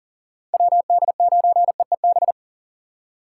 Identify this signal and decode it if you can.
Morse code, heard as a keyed tone, about 40 wpm, 700 Hz. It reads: WD9EEB
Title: USA Call Signs - 6 Characters - 2x Character Spacing - 40wpm